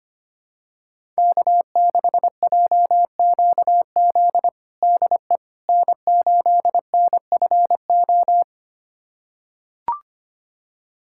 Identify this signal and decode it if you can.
Morse code, heard as a keyed tone, about 25 wpm, 700 Hz. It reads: K6JQZ DE N8NFO
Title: USA Callsign DE Callsign - 5 Characters - 25wpm